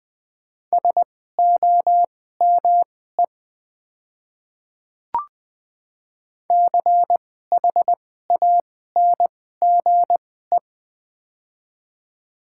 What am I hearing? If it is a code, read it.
Morse code, heard as a keyed tone, about 20 wpm, spaced as for 14 wpm, 700 Hz. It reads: SOME CHANGE
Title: 2x Character Spacing - Top 200 Words - 20wpm